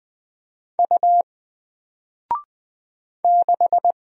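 Morse code, heard as a keyed tone, about 20 wpm, 700 Hz. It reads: U 6
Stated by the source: Single letter and number Morse drill at 20 wpm, from N4LCD.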